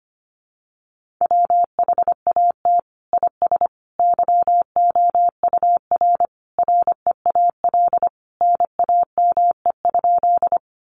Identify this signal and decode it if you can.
Morse code, heard as a keyed tone, about 25 wpm, 700 Hz. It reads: WHAT IS YOUR REAL NAME?
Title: Sentences from Top 300 Words 25wpm